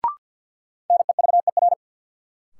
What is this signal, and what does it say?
Morse code, heard as a keyed tone, about 52 wpm, 700 Hz. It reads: NEVER
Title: Sentences from Top 300 Words - New Word Drill - 50wpm